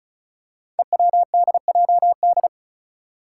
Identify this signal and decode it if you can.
Morse code, heard as a keyed tone, about 36 wpm, 700 Hz. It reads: EWDJD
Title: Five Challenging Letter-Number Combinations 35wpm